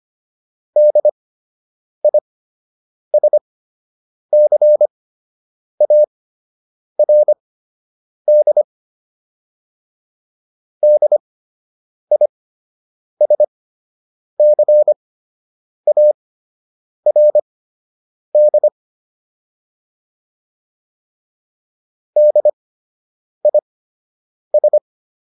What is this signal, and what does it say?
Morse code, heard as a keyed tone, about 25 wpm, spaced as for 8 wpm, 600 Hz. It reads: DISCARD DISCARD DIS